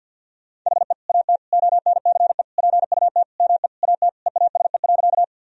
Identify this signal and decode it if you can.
Morse code, heard as a keyed tone, about 48 wpm, 700 Hz. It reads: HE AT ONCE PUT ME AT EASE.
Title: Sentences from Top 1000 Words - Review Part 1 - 50wpm